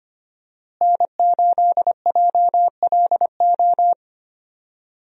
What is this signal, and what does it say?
Morse code, heard as a keyed tone, about 25 wpm, 700 Hz. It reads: N8JLO
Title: USA Call Signs - 5 Characters - Repeated 3x - 25wpm